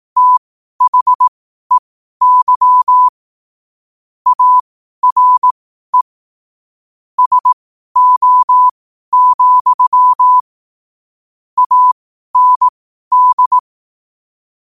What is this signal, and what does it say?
Morse code, heard as a keyed tone, about 18 wpm, spaced as for 12 wpm, 1000 Hz. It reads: THEY ARE SO, AND